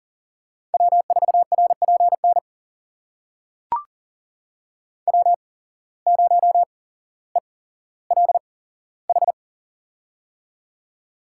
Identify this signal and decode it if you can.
Morse code, heard as a keyed tone, about 40 wpm, spaced as for 17 wpm, 700 Hz. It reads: W4RPN W0ELH
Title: USA Call Signs - 5 Characters - 8x Character Spacing - 40wpm